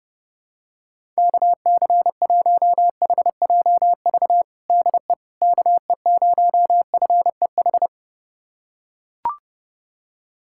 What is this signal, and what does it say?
Morse code, heard as a keyed tone, about 32 wpm, 700 Hz. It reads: KC1HJV DE KE0FEH
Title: USA Callsign DE Callsign 30wpm